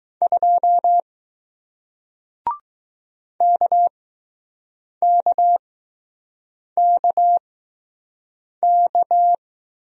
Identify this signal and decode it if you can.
Morse code, heard as a keyed tone, about 20 wpm, spaced as for 6 wpm, 700 Hz. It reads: 2 KKKK